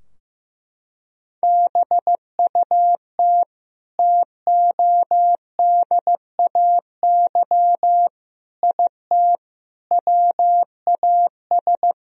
Morse code, heard as a keyed tone, about 15 wpm, 700 Hz. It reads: BUT TODAY IT WAS
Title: Sentences from Top 1000 Words - Review Part 4 - 15wpm